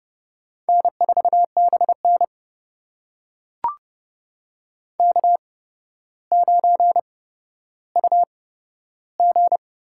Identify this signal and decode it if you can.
Morse code, heard as a keyed tone, about 30 wpm, spaced as for 13 wpm, 700 Hz. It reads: N4BN K9UG
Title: USA Call Signs - 4 Characters - 8x Character Spacing - 30wpm